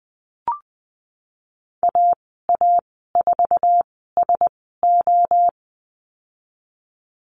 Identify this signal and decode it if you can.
Morse code, heard as a keyed tone, about 20 wpm, spaced as for 15 wpm, 700 Hz. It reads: AA4SO
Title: 048 Lesson 9g - Number 4 - Call Signs 20wpm